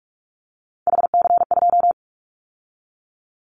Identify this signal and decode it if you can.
Morse code, heard as a keyed tone, about 46 wpm, 700 Hz. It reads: HC2
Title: Callsign Prefixes for Top 100 DXCC - 45wpm